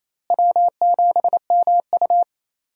Morse code, heard as a keyed tone, about 28 wpm, 700 Hz. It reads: W7MU
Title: USA Call Signs - 4 Characters - Repeated 2x - 28wpm